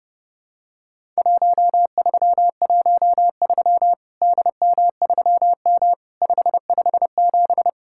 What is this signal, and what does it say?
Morse code, heard as a keyed tone, about 30 wpm, 700 Hz. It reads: 1313 DM3M 557
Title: Worked All Europe DX Contest - QTC Exchange - 30wpm